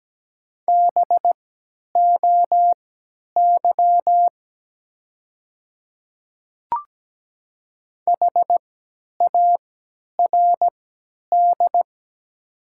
Morse code, heard as a keyed tone, about 17 wpm, spaced as for 10 wpm, 700 Hz. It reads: BOY HARD
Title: 3x Character Spacing - Top 300 Words - 17wpm